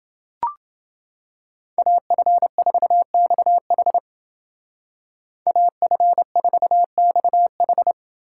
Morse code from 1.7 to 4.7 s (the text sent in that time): AF4XH